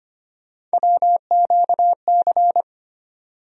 Morse code, heard as a keyed tone, about 25 wpm, 700 Hz. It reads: WQC
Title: USA Call Sign Postfixes - 3 Letters 25wpm